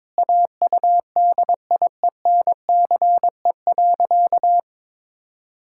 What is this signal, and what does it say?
Morse code, heard as a keyed tone, about 22 wpm, 700 Hz. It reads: AUDIENCE.
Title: Pangram Challenge v2 - 22wpm